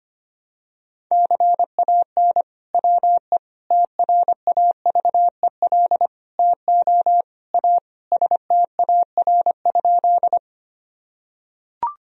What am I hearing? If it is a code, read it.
Morse code, heard as a keyed tone, about 25 wpm, 700 Hz. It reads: CAN WE TRAVEL TO A STAR?
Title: Sentences from Top 1000 Words - Review Part 2 - 25wpm